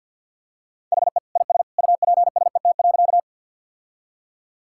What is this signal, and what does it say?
Morse code, heard as a keyed tone, about 51 wpm, 700 Hz. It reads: HE IS UPSET.
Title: Sentences from Top 1000 Words - Review Part 5 - 50wpm